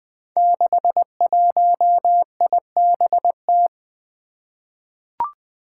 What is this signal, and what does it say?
Morse code, heard as a keyed tone, about 21 wpm, 700 Hz. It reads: 61IBT
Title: Five Challenging Letter-Number Combinations 20wpm